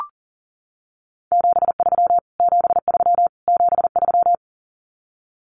Morse code, heard as a keyed tone, about 40 wpm, 700 Hz. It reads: 73 73 73
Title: Abbreviations Repeated 3x - 40wpm